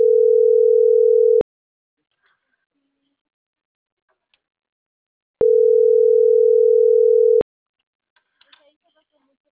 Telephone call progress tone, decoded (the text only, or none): ringback tone